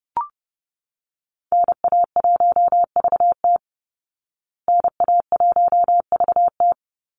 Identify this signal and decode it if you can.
Morse code, heard as a keyed tone, about 30 wpm, 700 Hz. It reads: NA1VT NA1VT
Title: Call Signs Repeated 2x - 30wpm